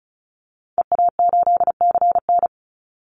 Morse code, heard as a keyed tone, about 34 wpm, 700 Hz. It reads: EA8CN
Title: Call Signs Repeated 3x - 35wpm